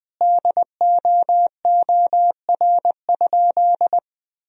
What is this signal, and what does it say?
Morse code, heard as a keyed tone, about 20 wpm, 700 Hz. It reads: DOOR?
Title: Sentences from Top 1000 Words - Review Part 4 - 20wpm